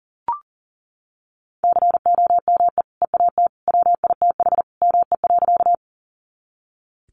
Morse code, heard as a keyed tone, about 40 wpm, 700 Hz. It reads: COME EAT WITH ME.
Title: Sentences from Top 300 Words 40wpm